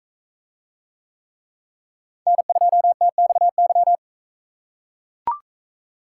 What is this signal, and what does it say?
Morse code, heard as a keyed tone, about 41 wpm, 700 Hz. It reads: N2TXY